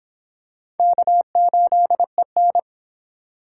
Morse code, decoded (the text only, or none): K8EN